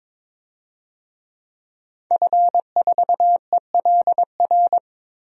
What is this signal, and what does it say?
Morse code, heard as a keyed tone, about 22 wpm, 700 Hz. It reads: F4ELR
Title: DXCC-100 Call Signs - 5 Characters - Repeated 3x - 22wpm